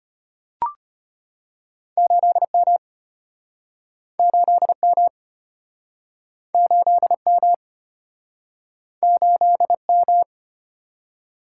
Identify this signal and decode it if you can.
Morse code, heard as a keyed tone, about 35 wpm, 700 Hz. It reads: 8M 8M 8M 8M